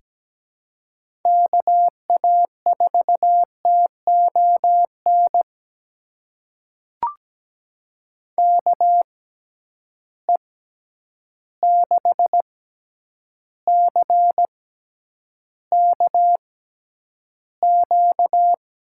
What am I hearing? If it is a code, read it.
Morse code, heard as a keyed tone, about 17 wpm, spaced as for 9 wpm, 700 Hz. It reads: KA4TON KE6CKQ
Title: USA Call Signs - 6 Characters - 6x Character Spacing - 17wpm